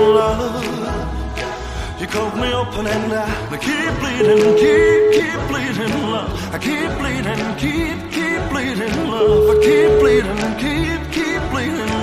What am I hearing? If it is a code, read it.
Ringback tone (Europe, 425 Hz), with noise.